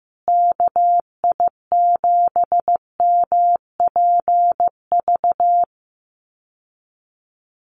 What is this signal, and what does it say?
Morse code, heard as a keyed tone, about 15 wpm, 700 Hz. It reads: KI7MPV
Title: USA Call Signs Repeated 3x 15wpm